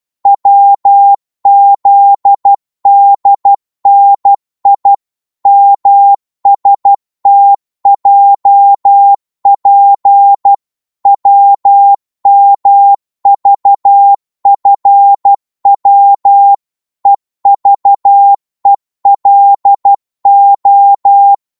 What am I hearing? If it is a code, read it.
Morse code, heard as a keyed tone, about 12 wpm, 800 Hz. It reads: WZDNI MSTJP WMVFW EVELO
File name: Morse / ltr-12wpm